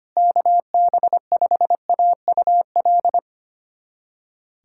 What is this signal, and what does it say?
Morse code, heard as a keyed tone, about 25 wpm, 700 Hz. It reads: KB5AUL
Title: USA Callsign DE Callsign - 6 Characters - 25wpm